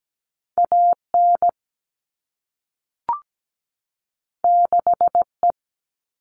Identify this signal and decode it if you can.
Morse code, heard as a keyed tone, about 17 wpm, 700 Hz. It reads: AN 6E
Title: All Cut Numbers - 2-Digit 17wpm